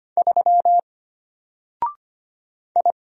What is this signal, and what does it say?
Morse code, heard as a keyed tone, about 25 wpm, 700 Hz. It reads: 3 I
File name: Single-Letter-Number-25wpm